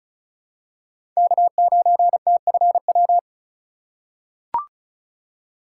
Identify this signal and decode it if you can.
Morse code, heard as a keyed tone, about 34 wpm, 700 Hz. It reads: K9TFW
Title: USA Call Signs - 5 Characters - 4x Character Spacing - 35wpm